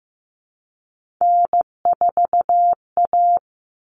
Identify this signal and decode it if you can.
Morse code, heard as a keyed tone, about 15 wpm, 700 Hz. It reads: N4A
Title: Common Cut Numbers - 3-Digit 15wpm